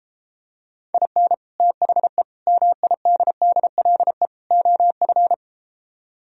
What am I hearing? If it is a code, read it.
Morse code, heard as a keyed tone, about 33 wpm, 700 Hz. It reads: IN THE MIDDLE OF